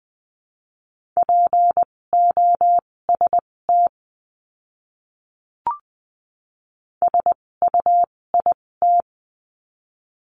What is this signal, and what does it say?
Morse code, heard as a keyed tone, about 20 wpm, spaced as for 16 wpm, 700 Hz. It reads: POST SUIT
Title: 256 Lesson 42a - Words 12-20wpm